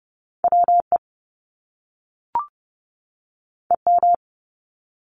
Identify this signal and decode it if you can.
Morse code, heard as a keyed tone, about 30 wpm, 700 Hz. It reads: WE EM
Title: Top Two Letter Combinations in Words 30wpm